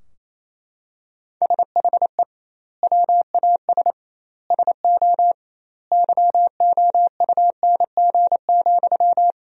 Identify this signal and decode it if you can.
Morse code, heard as a keyed tone, about 28 wpm, 700 Hz. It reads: SHE WAS SO YOUNG,
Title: 2x Word-Spacing - Sentences from Top 400 Words - 28wpm